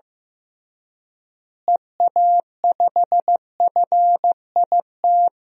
Morse code, heard as a keyed tone, about 15 wpm, 700 Hz. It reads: EA5FIT
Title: DXCC-100 Call Signs - 6 Characters - Repeated 3x - 15wpm